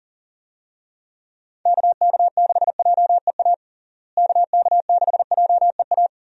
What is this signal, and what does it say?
Morse code, heard as a keyed tone, about 37 wpm, 700 Hz. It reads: KK6JEA KK6JEA
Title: USA Call Signs - 6 Characters - Repeated 2x - 40wpm